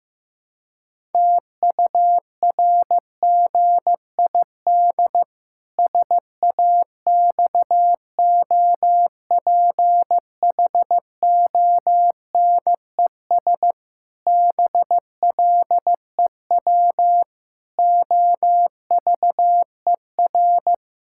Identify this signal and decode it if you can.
Morse code, heard as a keyed tone, about 15 wpm, 700 Hz. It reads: TURGID SAXOPHONES BLEW OVER
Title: Pangram Challenge v2 - 15wpm